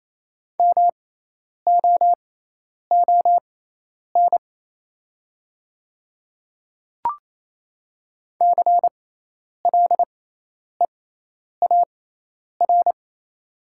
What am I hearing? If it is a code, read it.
Morse code, heard as a keyed tone, about 28 wpm, spaced as for 10 wpm, 700 Hz. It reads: MOON CLEAR
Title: 6x Character Spacing - Top 500 Words - 28wpm